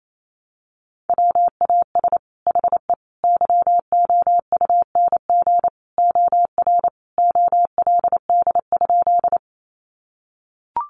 Morse code, heard as a keyed tone, about 28 wpm, 700 Hz. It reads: WAS HE YOUNG OR OLD?
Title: Sentences from Top 400 Words 28wpm